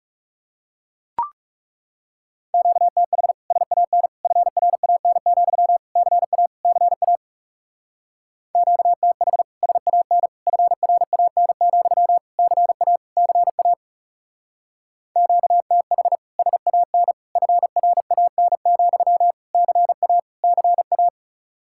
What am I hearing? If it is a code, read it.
Morse code, heard as a keyed tone, about 36 wpm, 700 Hz. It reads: QTH SAN FRAN, CA CA QTH SAN FRAN, CA CA QTH SAN FRAN, CA CA
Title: Speed-Racing Common QSO Phrases - 30wpm